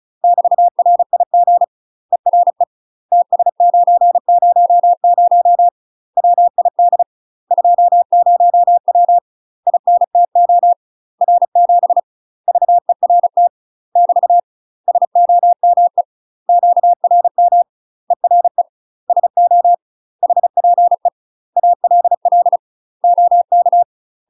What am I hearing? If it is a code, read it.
Morse code, heard as a keyed tone, about 36 wpm, 700 Hz. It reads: <BT> RIG ERE TS900 WID 20W INTO R7 VERT <BT> SOME QRM ERE SO HPE ALL OK
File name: UK MASTER QSOs TRAINING / 35WPM / UK MASTER QSOs 0017